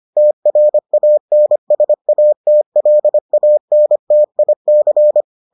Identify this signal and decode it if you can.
Morse code, heard as a keyed tone, about 25 wpm, 600 Hz. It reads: TRANSATLANTIC